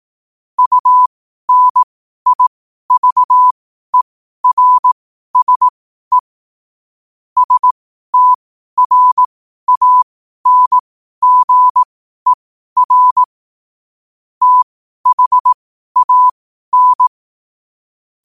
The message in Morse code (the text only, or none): UNIVERSE STRANGER THAN